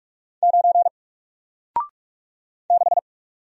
Morse code, heard as a keyed tone, about 46 wpm, 700 Hz. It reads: 9 6
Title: Single Letter-Number Alt 45wpm